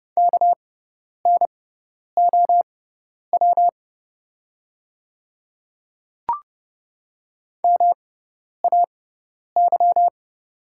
Morse code, heard as a keyed tone, about 31 wpm, spaced as for 10 wpm, 700 Hz. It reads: KNOW MAY
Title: 6x Character Spacing - Top 100 Words - 30wpm